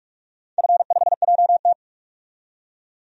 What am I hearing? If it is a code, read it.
Morse code, heard as a keyed tone, about 46 wpm, 700 Hz. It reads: F5JT
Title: DXCC-100 Call Signs - 4 Characters - 45wpm